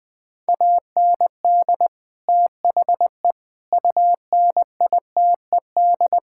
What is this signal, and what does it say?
Morse code, heard as a keyed tone, about 20 wpm, 700 Hz. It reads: AND THE UNITED
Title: Sentences with 4-Word Phrases - Spoken Beginning - 20wpm